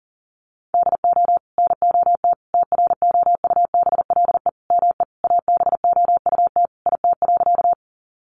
Morse code, heard as a keyed tone, about 40 wpm, 700 Hz. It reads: DO NOT TROUBLE ME ABOUT IT.
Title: Sentences from Top 1200 Words 40wpm